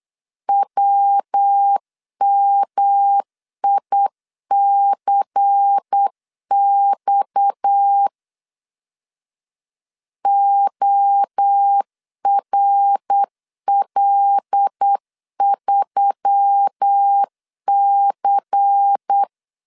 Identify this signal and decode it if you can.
Morse code, heard as a keyed tone, about 8 wpm, 800 Hz. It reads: WMICX ORL3C
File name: CW / DET / 1 6 1500C S B 8wpm 5142